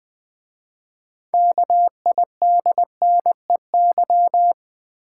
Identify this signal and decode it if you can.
Morse code, heard as a keyed tone, about 20 wpm, 700 Hz. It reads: KIDNEY
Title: 6 Letter Words - First 3 Letters Given - 20wpm